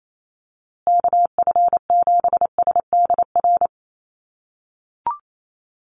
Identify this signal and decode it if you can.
Morse code, heard as a keyed tone, about 28 wpm, 700 Hz. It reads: KF7SDR
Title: USA Call Signs Repeated 2x 28wpm